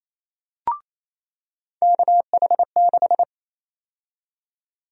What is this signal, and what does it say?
Morse code, heard as a keyed tone, about 28 wpm, 700 Hz. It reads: KH6
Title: USA Call Sign Prefixes - 28wpm